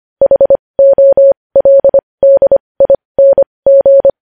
Morse code, heard as a keyed tone, about 25 wpm, spaced as for 20 wpm, 550 Hz. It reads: HOLDING
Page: News Headlines in Morse Code at 20 WPM